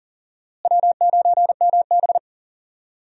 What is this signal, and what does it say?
Morse code, heard as a keyed tone, about 40 wpm, 700 Hz. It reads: W9MB
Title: USA Call Signs - 4 Characters - Repeated 3x - 40wpm